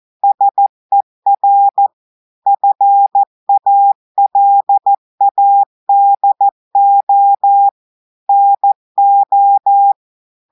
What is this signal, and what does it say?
Morse code, heard as a keyed tone, about 14 wpm, 800 Hz. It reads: SER FALADO NO
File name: audios / portugais / portugais texte en clair 2 vitesse 14 mots minutes